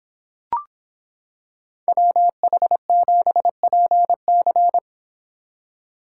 Morse code, heard as a keyed tone, about 26 wpm, 700 Hz. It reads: WH7PC